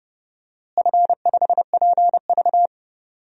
Morse code, heard as a keyed tone, about 30 wpm, 700 Hz. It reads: F5PV
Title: DXCC-100 Call Signs - 4 Characters - Repeated 2x - 30wpm